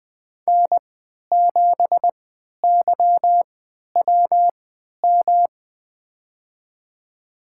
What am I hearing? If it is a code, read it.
Morse code, heard as a keyed tone, about 20 wpm, spaced as for 11 wpm, 700 Hz. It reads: N7YWM
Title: USA Call Signs - 5 Characters - 3x Character Spacing - 20wpm